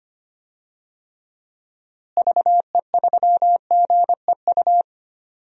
Morse code, heard as a keyed tone, about 25 wpm, 700 Hz. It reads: VE3GEU
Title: DXCC-100 Call Signs - 6 Characters - Repeated 3x - 25wpm